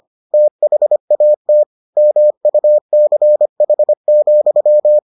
Morse code, heard as a keyed tone, about 25 wpm, 600 Hz. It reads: THAT MUCH,